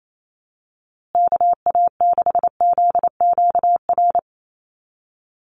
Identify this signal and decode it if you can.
Morse code, heard as a keyed tone, about 28 wpm, 700 Hz. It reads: KA6ZQR
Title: USA Call Signs Repeated 3x 28wpm